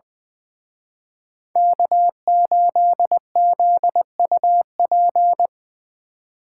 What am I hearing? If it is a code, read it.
Morse code, heard as a keyed tone, about 23 wpm, 700 Hz. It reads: K8ZUP
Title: Speed Racing - USA Call Signs 20wpm